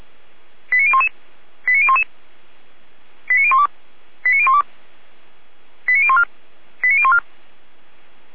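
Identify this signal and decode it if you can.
Five-tone selective call (ZVEI), heard as quick runs of five tones, tones of about 70 ms. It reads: 89011 89011 89012 89012 89014 89014
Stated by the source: ZVEI five-tone selective call 89010 received on SDR, from Feuerwehr Orlamünde fire brigade server.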